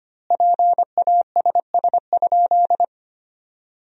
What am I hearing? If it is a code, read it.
Morse code, heard as a keyed tone, about 25 wpm, 700 Hz. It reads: PASS?
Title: 4x Word-Spacing - Sentences from Top 400 Words - 25wpm